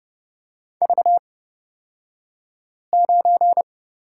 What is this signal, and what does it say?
Morse code, heard as a keyed tone, about 30 wpm, spaced as for 5 wpm, 690 Hz. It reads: V9
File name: Lesson 36d - BT - Two Characters 30wpm